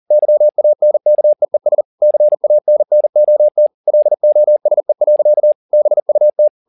Morse code, heard as a keyed tone, about 40 wpm, 600 Hz. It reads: YANKEES CANNOT LOSE. BUT